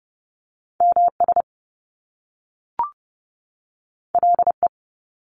Morse code, heard as a keyed tone, about 30 wpm, 700 Hz. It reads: MS LE
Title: Top Two Letter Combinations in Words 30wpm